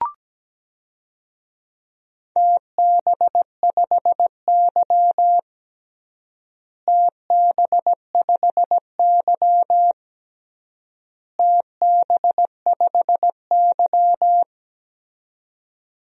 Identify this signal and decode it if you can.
Morse code, heard as a keyed tone, about 17 wpm, 700 Hz. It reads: TB5Y TB5Y TB5Y
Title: DXCC-100 Call Signs - 4 Characters - Repeated 3x - 17wpm